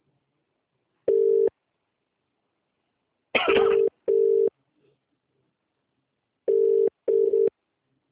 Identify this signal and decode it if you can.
Ringback tone (UK), 400 + 450 Hz.